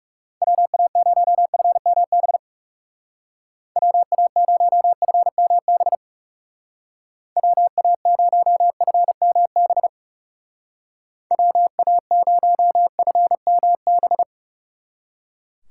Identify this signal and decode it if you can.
Morse code, heard as a keyed tone, about 37 wpm, 700 Hz. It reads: WA0FMB WA0FMB WA0FMB WA0FMB
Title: Speed Racing - USA 2x3 Call Signs 30wpm